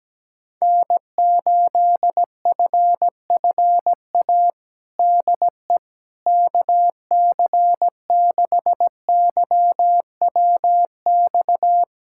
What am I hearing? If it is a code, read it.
Morse code, heard as a keyed tone, about 17 wpm, 700 Hz. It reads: N8FFA DE KC6YWX
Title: USA Callsign DE Callsign 17wpm